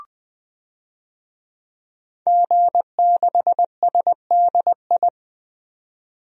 Morse code, heard as a keyed tone, about 20 wpm, 700 Hz. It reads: G6SDI
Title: DXCC-100 Call Signs - 5 Characters - Repeated 2x - 20wpm